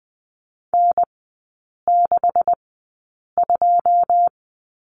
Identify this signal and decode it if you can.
Morse code, heard as a keyed tone, about 20 wpm, spaced as for 8 wpm, 690 Hz. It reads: N62
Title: 133 Lesson 22e - Question Mark - Three Characters 20wpm